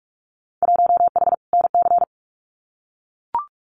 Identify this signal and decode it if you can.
Morse code, heard as a keyed tone, about 44 wpm, 700 Hz. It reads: 1H NC
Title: Winter Field Day Exchanges 45wpm